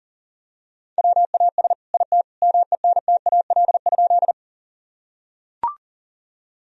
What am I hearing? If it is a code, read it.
Morse code, heard as a keyed tone, about 37 wpm, 700 Hz. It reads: WAS IT MENTAL?
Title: Sentences from Top 1000 Words - Review Part 4 - 40wpm